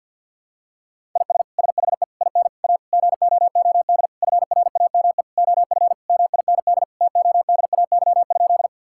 Morse code, heard as a keyed tone, about 50 wpm, 700 Hz. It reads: IS SHE IN A GOOD FRAME OF MIND TODAY?
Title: Sentences from Top 1000 Words - Review Part 6 - 50wpm